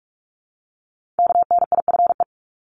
Morse code, heard as a keyed tone, about 46 wpm, 700 Hz. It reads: KNIFE
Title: Multi-letter Phonemes Repeated 3x 45wpm